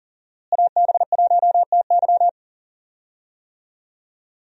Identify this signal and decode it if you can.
Morse code, heard as a keyed tone, about 40 wpm, 700 Hz. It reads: AB1TY
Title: USA Call Signs - 5 Characters - Repeated 2x - 40wpm